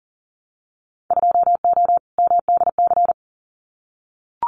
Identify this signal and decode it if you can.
Morse code, heard as a keyed tone, about 40 wpm, 700 Hz. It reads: 2O MDC
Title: Winter Field Day Exchanges Encore 40wpm